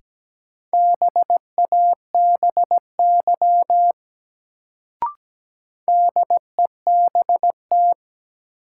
Morse code, heard as a keyed tone, about 17 wpm, 700 Hz. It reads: BABY DEBT
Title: Sentences from Top 1300 Words - New Word Drill Alt - 17wpm